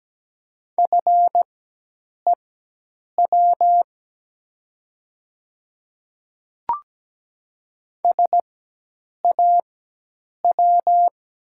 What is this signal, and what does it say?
Morse code, heard as a keyed tone, about 17 wpm, spaced as for 8 wpm, 700 Hz. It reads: FEW SAW
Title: 4x Character Spacing - Top 300 Words - 17wpm